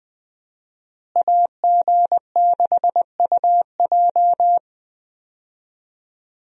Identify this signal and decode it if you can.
Morse code, heard as a keyed tone, about 20 wpm, 700 Hz. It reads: AG6UJ
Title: USA Call Signs - 5 Characters - Repeated 2x - 20wpm